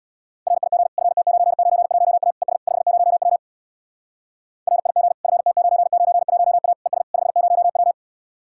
Morse code, heard as a keyed tone, about 77 wpm, 700 Hz. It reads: RIG FT991A ES 50W RIG FT991A ES 50W